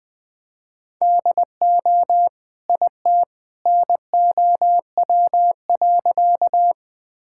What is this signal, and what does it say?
Morse code, heard as a keyed tone, about 20 wpm, 700 Hz. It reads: DO IT NOW.
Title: Sentences with 2-Word Phrases - 20wpm